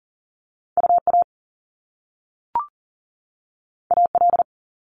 Morse code, heard as a keyed tone, about 40 wpm, 700 Hz. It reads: UA AL
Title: Top Two Letter Combinations in Words 40wpm